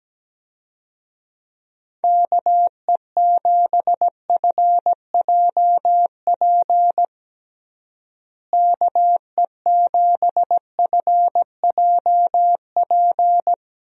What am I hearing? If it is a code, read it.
Morse code, heard as a keyed tone, about 17 wpm, 700 Hz. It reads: KE7FJP KE7FJP
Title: USA Call Signs - 6 Characters - Repeated 3x - 17wpm